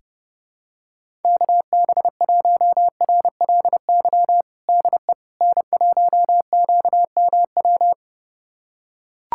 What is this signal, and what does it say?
Morse code, heard as a keyed tone, about 32 wpm, 700 Hz. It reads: KB1RLY DE N1QMW
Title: USA Callsign DE Callsign 30wpm